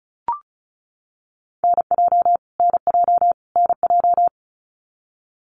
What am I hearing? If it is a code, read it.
Morse code, heard as a keyed tone, about 34 wpm, 700 Hz. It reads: NJ NJ NJ
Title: US States Repeated 3x 35wpm